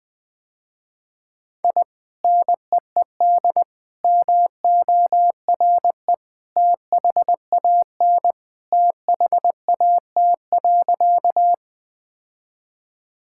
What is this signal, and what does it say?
Morse code, heard as a keyed tone, about 20 wpm, 700 Hz. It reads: I NEED MORE THAN THAT.
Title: Sentences with 2-Word Phrases - Spoken Beginning - 20wpm